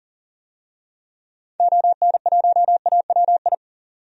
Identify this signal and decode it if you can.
Morse code, heard as a keyed tone, about 40 wpm, 700 Hz. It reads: ON1AWI
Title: DXCC-100 Call Signs - 6 Characters - 40wpm